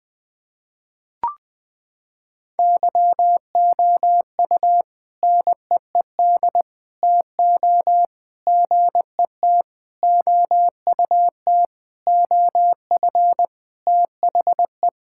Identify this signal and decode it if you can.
Morse code, heard as a keyed tone, about 20 wpm, 700 Hz. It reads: YOU NEED TO GET OUT OF THE
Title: Sentences with 4-Word Phrases - 20wpm